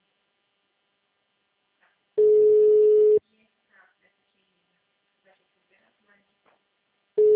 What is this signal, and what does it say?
Ringback tone (Europe, 425 Hz), with speech.